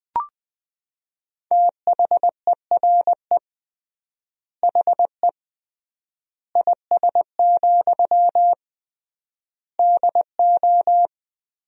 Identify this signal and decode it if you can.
Morse code, heard as a keyed tone, about 19 wpm, 700 Hz. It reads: THERE HE IS, DO
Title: 3x Word-Spacing - Sentences from Top 100 Words - 20wpm